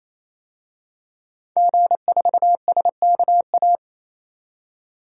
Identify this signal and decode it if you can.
Morse code, heard as a keyed tone, about 27 wpm, 700 Hz. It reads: G4SKA
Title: DXCC-100 Call Signs - 5 Characters - 28wpm